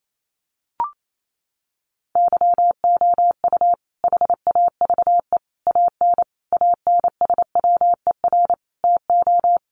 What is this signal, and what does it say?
Morse code, heard as a keyed tone, about 28 wpm, 700 Hz. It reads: YOU HAVE AN ANSWER TO
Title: Sentences from Top 700 Words 28wpm